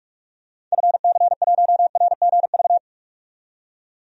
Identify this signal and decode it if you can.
Morse code, heard as a keyed tone, about 48 wpm, 700 Hz. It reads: FC1RGV
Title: DXCC-100 Call Signs - 6 Characters - 45wpm